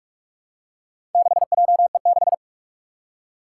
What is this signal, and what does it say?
Morse code, heard as a keyed tone, about 46 wpm, 700 Hz. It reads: 6JE6
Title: Four Challenging Letter-Number Combinations 45wpm